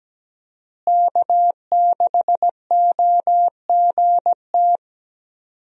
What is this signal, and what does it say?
Morse code, heard as a keyed tone, about 17 wpm, 700 Hz. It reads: K6OGT
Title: USA Call Signs - 5 Characters - Repeated 3x - 17wpm